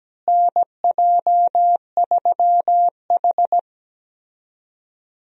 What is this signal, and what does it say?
Morse code, heard as a keyed tone, about 17 wpm, 700 Hz. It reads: NJ3H
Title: USA Call Signs - 4 Characters - Repeated 3x - 17wpm